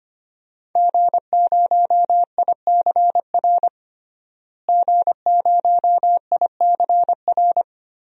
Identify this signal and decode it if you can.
Morse code, heard as a keyed tone, about 25 wpm, 700 Hz. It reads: G0ICR G0ICR
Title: DXCC-100 Call Signs - 5 Characters - Repeated 3x - 25wpm